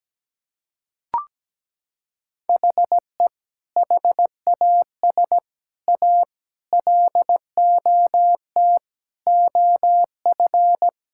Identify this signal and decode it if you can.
Morse code, heard as a keyed tone, about 17 wpm, 700 Hz. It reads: HE HAS A LOT OF
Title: Sentences from Top 1000 Words - Review Part 6 - 17wpm